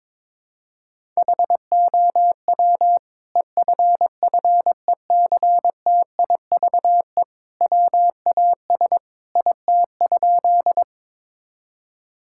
Morse code, heard as a keyed tone, about 22 wpm, 700 Hz. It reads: HOW EFFECTIVE WAS IT?
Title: Sentences from Top 1000 Words - Review Part 6 - 22wpm